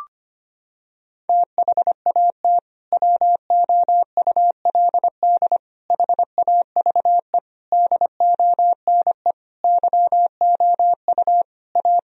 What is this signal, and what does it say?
Morse code, heard as a keyed tone, about 25 wpm, 700 Hz. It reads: THAT WOULD HAVE DONE YOU A
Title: Sentences from Top 1000 Words - Review Part 3 - 25wpm